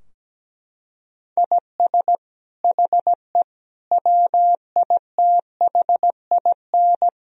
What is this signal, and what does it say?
Morse code, heard as a keyed tone, about 17 wpm, 700 Hz. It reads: IS HE WITHIN
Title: Sentences from Top 1000 Words - Review Part 4 - 17wpm